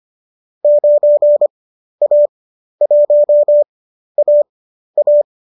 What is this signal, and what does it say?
Morse code, heard as a keyed tone, about 25 wpm, spaced as for 12 wpm, 590 Hz. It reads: 9A1AA
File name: qso012 12